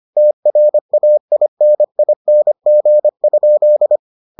Morse code, heard as a keyed tone, about 25 wpm, 600 Hz. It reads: TRAINING?